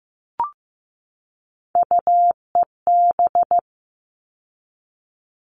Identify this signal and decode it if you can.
Morse code, heard as a keyed tone, about 15 wpm, 700 Hz. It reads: UEB
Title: All Cut Numbers - 3-Digit 15wpm